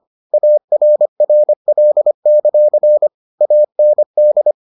Morse code, heard as a keyed tone, about 25 wpm, 600 Hz. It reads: ARRL; AND